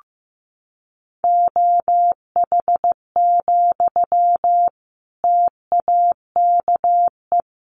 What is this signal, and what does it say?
Morse code, heard as a keyed tone, about 15 wpm, 700 Hz. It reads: OH, TAKE
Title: Sentences from Top 500 Words 15wpm